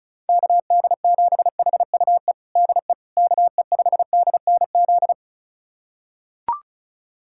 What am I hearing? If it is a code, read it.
Morse code, heard as a keyed tone, about 34 wpm, 700 Hz. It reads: KD7HUE DE KE5DNZ
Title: USA Callsign DE Callsign - 6 Characters - 35wpm